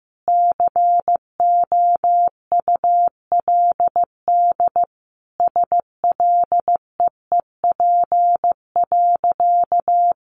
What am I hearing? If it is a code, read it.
Morse code, heard as a keyed tone, about 15 wpm, 700 Hz. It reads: COULD SLEEP.
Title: Sentences from Top 500 Words 15wpm